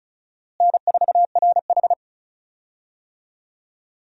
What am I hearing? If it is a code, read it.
Morse code, heard as a keyed tone, about 36 wpm, 700 Hz. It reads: N4RH
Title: USA Call Signs - 4 Characters - Repeated 2x - 35wpm